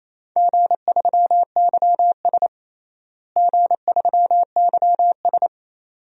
Morse code, heard as a keyed tone, about 28 wpm, 700 Hz. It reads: G3YS G3YS
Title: DXCC-100 Call Signs - 4 Characters - Repeated 2x - 28wpm